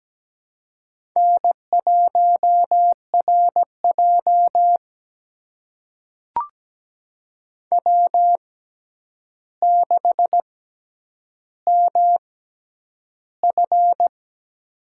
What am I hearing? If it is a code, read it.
Morse code, heard as a keyed tone, about 17 wpm, spaced as for 9 wpm, 700 Hz. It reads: N1RJ W6MF
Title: USA Call Signs - 4 Characters - 6x Character Spacing - 17wpm